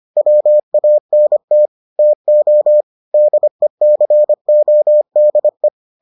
Morse code, heard as a keyed tone, about 25 wpm, 600 Hz. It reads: WANT TO DECODE